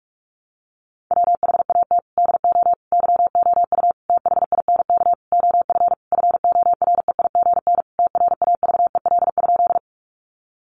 Morse code, heard as a keyed tone, about 44 wpm, 700 Hz. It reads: WHAT DO YOU THINK OF FOREIGN TRAVEL?
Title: Sentences from Top 1400 Words 45wpm